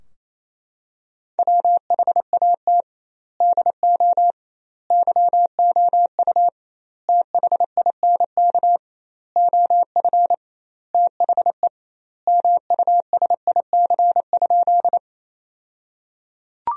Morse code, heard as a keyed tone, about 28 wpm, 700 Hz. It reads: WHAT DO YOU THINK OF THE MUSIC?
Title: 2x Word-Spacing - Sentences from Top 300 Words - 28wpm